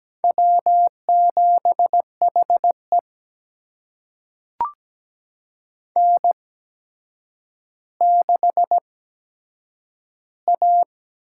Morse code, heard as a keyed tone, about 17 wpm, spaced as for 8 wpm, 700 Hz. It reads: W7HE N6A
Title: USA Call Signs - 4 Characters - 8x Character Spacing - 17wpm